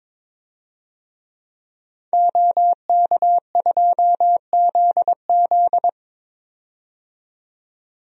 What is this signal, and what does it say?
Morse code, heard as a keyed tone, about 22 wpm, 700 Hz. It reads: OK2ZZ
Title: DXCC-100 Call Signs - 5 Characters - 22wpm